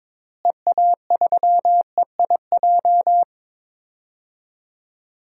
Morse code, heard as a keyed tone, about 22 wpm, 700 Hz. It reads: EA3EIJ
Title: DXCC-100 Call Signs - 6 Characters - Repeated 3x - 22wpm